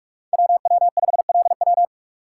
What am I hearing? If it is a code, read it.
Morse code, heard as a keyed tone, about 42 wpm, 700 Hz. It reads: WW5LW
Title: USA Call Signs - 3x Character Spacing - 45wpm